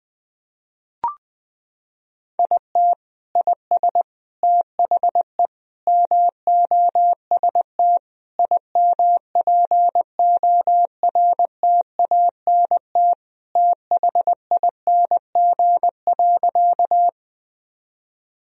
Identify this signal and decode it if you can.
Morse code, heard as a keyed tone, about 20 wpm, 700 Hz. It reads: IT IS THE MOST IMPORTANT THING.
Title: Sentences from Top 1000 Words - Review Part 1 - 20wpm